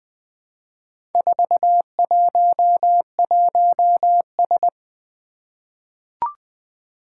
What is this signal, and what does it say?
Morse code, heard as a keyed tone, about 20 wpm, 700 Hz. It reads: 411S
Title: Four Challenging Letter-Number Combinations 20wpm